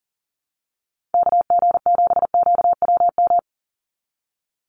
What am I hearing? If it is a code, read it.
Morse code, heard as a keyed tone, about 40 wpm, 700 Hz. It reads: KG7QWM
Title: USA Call Signs 40wpm